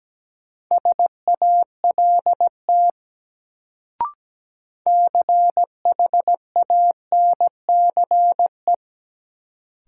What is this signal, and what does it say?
Morse code, heard as a keyed tone, about 17 wpm, 700 Hz. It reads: SALT CHANCE
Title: Sentences from Top 800 Words - New Word Drill - 17wpm